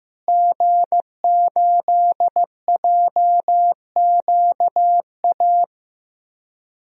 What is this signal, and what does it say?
Morse code, heard as a keyed tone, about 15 wpm, 700 Hz. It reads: G8JQA
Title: DXCC-100 Call Signs - 5 Characters - Repeated 3x - 15wpm